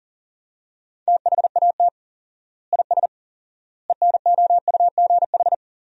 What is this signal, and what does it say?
Morse code, heard as a keyed tone, about 37 wpm, 700 Hz. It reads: THAT IS ENOUGH
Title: 4x Word-Spacing - Sentences from Top 400 Words - 40wpm